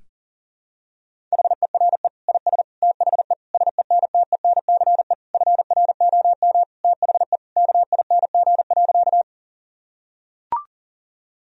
Morse code, heard as a keyed tone, about 40 wpm, 700 Hz. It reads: HERE IS THE SENTENCE FROM THE KING.
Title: Sentences from Top 1000 Words - Review Part 1 - 40wpm